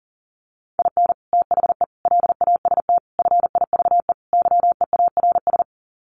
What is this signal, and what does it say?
Morse code, heard as a keyed tone, about 40 wpm, 700 Hz. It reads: IN THE LAST FIVE YEARS
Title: Sets of 5 Words Encore 40wpm